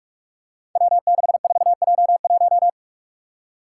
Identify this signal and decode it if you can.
Morse code, heard as a keyed tone, about 46 wpm, 700 Hz. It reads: W64J1
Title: Five Challenging Letter-Number Combinations 45wpm